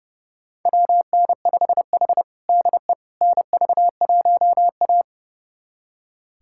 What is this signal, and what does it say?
Morse code, heard as a keyed tone, about 30 wpm, 700 Hz. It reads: WN5H DE NV1A
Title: USA Callsign DE Callsign - 4 Characters - 30wpm